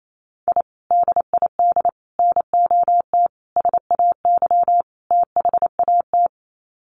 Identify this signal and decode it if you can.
Morse code, heard as a keyed tone, about 28 wpm, 700 Hz. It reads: I DID NOT SAY THAT
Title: Sets of 5 Words Encore2 28wpm